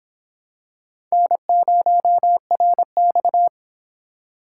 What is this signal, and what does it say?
Morse code, heard as a keyed tone, about 25 wpm, 700 Hz. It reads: N0RX